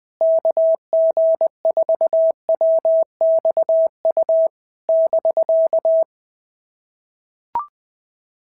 Morse code, heard as a keyed tone, about 19 wpm, 650 Hz. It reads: KG4WXU <BK>
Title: POTA QSO Practice - 20wpm